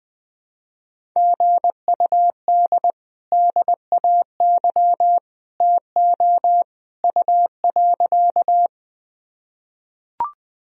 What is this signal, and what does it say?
Morse code, heard as a keyed tone, about 20 wpm, 700 Hz. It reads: GUD DAY TO U.
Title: Sentences from Top 200 Words and Abbreviations 20wpm